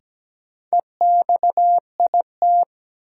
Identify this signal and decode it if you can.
Morse code, heard as a keyed tone, about 17 wpm, 700 Hz. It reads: EXIT
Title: Sentences from Top 1100 Words - New Word Drill - 17wpm